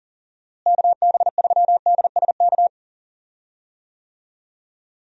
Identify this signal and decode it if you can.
Morse code, heard as a keyed tone, about 38 wpm, 700 Hz. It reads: KB3DSK
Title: USA Call Signs - 6 Characters - Repeated 2x - 40wpm